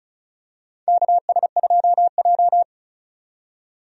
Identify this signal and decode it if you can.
Morse code, heard as a keyed tone, about 36 wpm, 700 Hz. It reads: KS2J
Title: USA Call Signs - 4 Characters - 35wpm